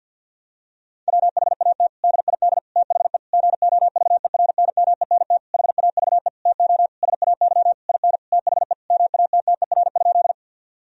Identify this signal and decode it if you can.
Morse code, heard as a keyed tone, about 50 wpm, 700 Hz. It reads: WHAT DID THE GOVERNMENT HAVE TO SAY IN THE MATTER?
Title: Sentences from Top 1000 Words - Review Part 3 - 50wpm